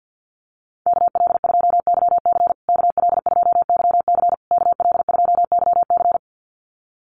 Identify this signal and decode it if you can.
Morse code, heard as a keyed tone, about 50 wpm, 700 Hz. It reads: KL2YC KL2YC KL2YC
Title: USA Call Signs Repeated 3x 50wpm